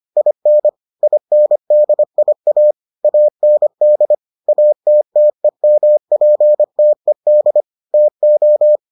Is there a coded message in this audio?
Yes: Morse code from 0.2 to 8.8 s, heard as a keyed tone, about 25 wpm, 600 Hz. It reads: IN INDIA AND ATTEMPTED TO